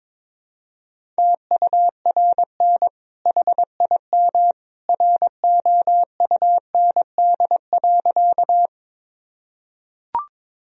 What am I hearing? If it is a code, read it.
Morse code, heard as a keyed tone, about 22 wpm, 700 Hz. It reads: TURN HIM ROUND.
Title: Sentences from Top 1000 Words - Review Part 1 - 22wpm